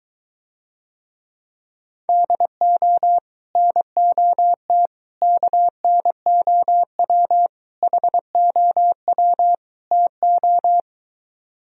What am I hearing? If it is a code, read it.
Morse code, heard as a keyed tone, about 23 wpm, 700 Hz. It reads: DO NOT KNOW HOW TO